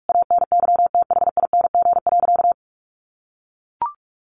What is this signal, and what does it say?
Morse code, heard as a keyed tone, about 44 wpm, 700 Hz. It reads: ANYTHING.
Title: Sentences from Top 1000 Words 45wpm